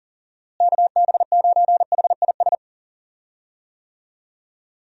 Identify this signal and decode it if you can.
Morse code, heard as a keyed tone, about 40 wpm, 700 Hz. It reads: KB9HIS
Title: USA Call Signs - 6 Characters - Repeated 2x - 40wpm